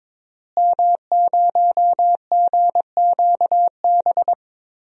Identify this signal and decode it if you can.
Morse code, heard as a keyed tone, about 22 wpm, 700 Hz. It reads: M0GQB
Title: DXCC-100 Call Signs - 5 Characters - Repeated 3x - 22wpm